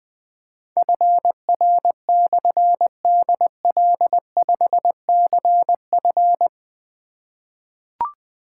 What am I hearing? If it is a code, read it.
Morse code, heard as a keyed tone, about 21 wpm, 700 Hz. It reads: FR/DL5CF
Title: Difficult Call Signs 20wpm